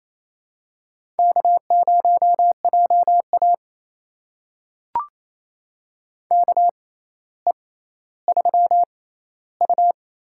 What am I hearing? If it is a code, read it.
Morse code, heard as a keyed tone, about 28 wpm, spaced as for 14 wpm, 700 Hz. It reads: K0JA KE3U
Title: USA Call Signs - 4 Characters - 6x Character Spacing - 28wpm